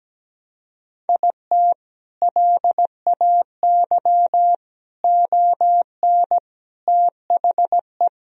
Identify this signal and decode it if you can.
Morse code, heard as a keyed tone, about 17 wpm, 700 Hz. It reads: IT LAY ON THE
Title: Sentences from Top 1000 Words - Review Part 2 - 17wpm